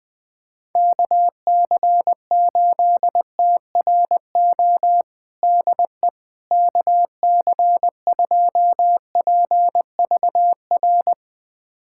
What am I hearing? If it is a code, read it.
Morse code, heard as a keyed tone, about 20 wpm, 700 Hz. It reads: KC8TRO DE KC2PVR
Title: USA Callsign DE Callsign - 6 Characters - 20wpm